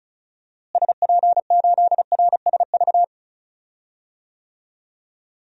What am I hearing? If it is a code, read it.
Morse code, heard as a keyed tone, about 36 wpm, 700 Hz. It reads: SP8RSV
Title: DXCC-100 Call Signs - 6 Characters - Repeated 2x - 35wpm